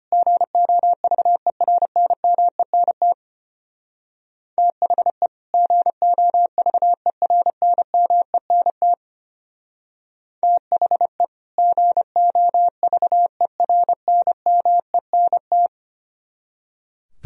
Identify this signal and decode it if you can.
Morse code, heard as a keyed tone, about 33 wpm, 700 Hz. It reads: GOVERNMENT THE GOVERNMENT THE GOVERNMENT